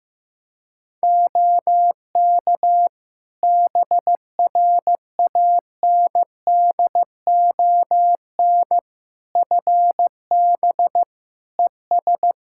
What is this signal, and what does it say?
Morse code, heard as a keyed tone, about 15 wpm, 700 Hz. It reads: OK BRANDON FB ES
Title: Practice QSO - Exchange 2 - Info - 15wpm